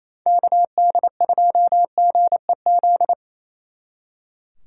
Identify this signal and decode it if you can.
Morse code, heard as a keyed tone, about 27 wpm, 700 Hz. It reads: KD2GEZ
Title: Speed Racing - USA 2x3 Call Signs 28wpm